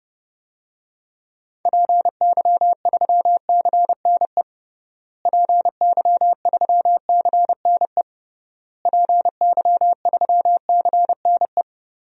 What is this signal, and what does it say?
Morse code, heard as a keyed tone, about 30 wpm, 700 Hz. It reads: PY3CNE PY3CNE PY3CNE
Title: DXCC-100 Call Signs - 6 Characters - Repeated 3x - 30wpm